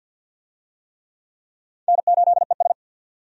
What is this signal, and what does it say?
Morse code, heard as a keyed tone, about 48 wpm, 700 Hz. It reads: N8ES
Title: USA Call Signs - 4 Characters - 50wpm